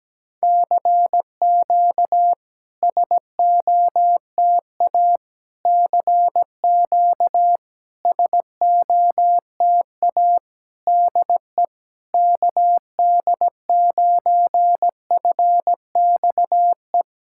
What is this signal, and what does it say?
Morse code, heard as a keyed tone, about 17 wpm, 690 Hz. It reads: CQ SOTA CQ SOTA DE KD9FXE
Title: SOTA QSO Practice 17wpm